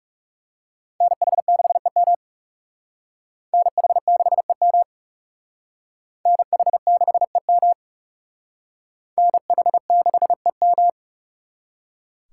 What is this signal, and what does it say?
Morse code, heard as a keyed tone, about 35 wpm, 700 Hz. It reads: NH6EM NH6EM NH6EM NH6EM
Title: Speed Racing - USA Call Signs 30wpm